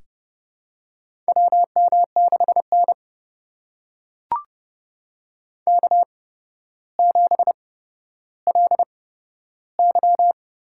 Morse code, heard as a keyed tone, about 30 wpm, spaced as for 13 wpm, 700 Hz. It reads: WM6N K7LY
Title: USA Call Signs - 4 Characters - 8x Character Spacing - 30wpm